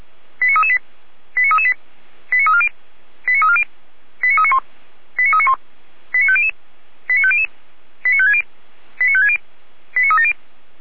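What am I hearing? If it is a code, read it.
Five-tone selective call (ZVEI), heard as quick runs of five tones, tones of about 70 ms. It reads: 89338 89338 89340 89340 89391 89391 89500 89500 89560 89560 89370
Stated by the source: ZVEI five-tone selective call received on SDR.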